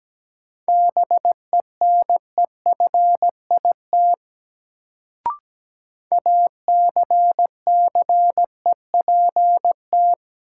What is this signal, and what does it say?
Morse code, heard as a keyed tone, about 17 wpm, 700 Hz. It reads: BENEFIT ACCEPT
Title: Sentences from Top 1300 Words - New Word Drill Alt - 17wpm